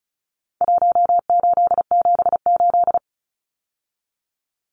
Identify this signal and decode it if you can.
Morse code, heard as a keyed tone, about 34 wpm, 700 Hz. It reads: 1878
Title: Numbers - Contest Style - 4-Digit 35wpm